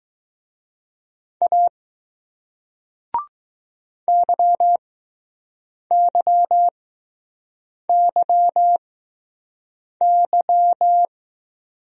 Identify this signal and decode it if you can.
Morse code, heard as a keyed tone, about 20 wpm, spaced as for 6 wpm, 700 Hz. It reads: A YYYY